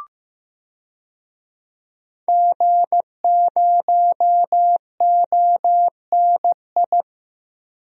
Morse code, heard as a keyed tone, about 15 wpm, 700 Hz. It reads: G0ONI
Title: DXCC-100 Call Signs - 5 Characters - 15wpm